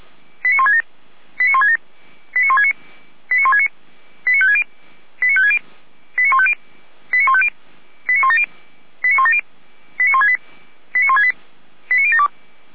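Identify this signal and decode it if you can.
Five-tone selective call (ZVEI), heard as quick runs of five tones, tones of about 80 ms. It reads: 89167 89167 89169 89169 89560 89560 89150 89150 89170 89170 89168 89168 89072
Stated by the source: Five-tone ZVEI selective call received by SDR.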